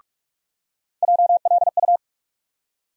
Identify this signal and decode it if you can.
Morse code, heard as a keyed tone, about 45 wpm, 700 Hz. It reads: JLU
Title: USA Call Sign Postfixes - 3 Letters 45wpm